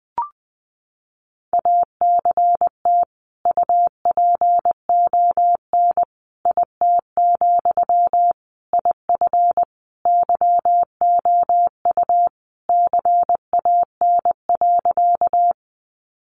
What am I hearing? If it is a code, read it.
Morse code, heard as a keyed tone, about 20 wpm, 700 Hz. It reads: ACT UPON IT, IF YOU CAN.
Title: Sentences from Top 1100 Words 20wpm